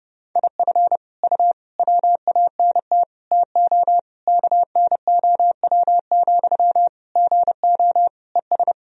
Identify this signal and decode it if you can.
Morse code, heard as a keyed tone, about 32 wpm, 700 Hz. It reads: IF U WANT TO KNOW, GO ES
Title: Sentences from Top 200 Words and Abbreviations 30wpm